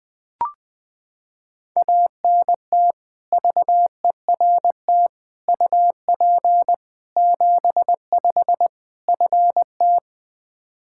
Morse code, heard as a keyed tone, about 20 wpm, 700 Hz. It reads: ANT VERT UP 75 FT
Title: K1USN SST QSO Practice 20wpm